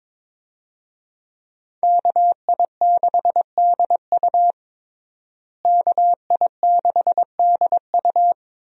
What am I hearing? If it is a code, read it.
Morse code, heard as a keyed tone, about 22 wpm, 700 Hz. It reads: KI6DU KI6DU
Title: USA Call Signs - 5 Characters - Repeated 2x - 22wpm